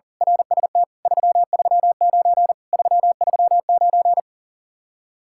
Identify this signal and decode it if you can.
Morse code, heard as a keyed tone, about 38 wpm, 700 Hz. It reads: RST 339 339
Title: Speed-Racing Common QSO Phrases - 40wpm